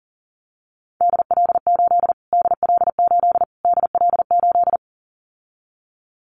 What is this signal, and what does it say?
Morse code, heard as a keyed tone, about 40 wpm, 700 Hz. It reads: DL8 DL8 DL8
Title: Callsign Prefixes for Top 25 DXCC - Repeated 3x - 40wpm